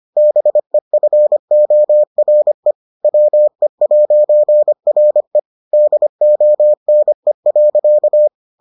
Morse code, heard as a keyed tone, about 25 wpm, 600 Hz. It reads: BEFORE WE'RE DONE.